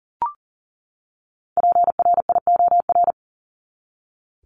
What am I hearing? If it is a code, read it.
Morse code, heard as a keyed tone, about 40 wpm, 700 Hz. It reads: PRIOR
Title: Top 1501-1750 Words 40wpm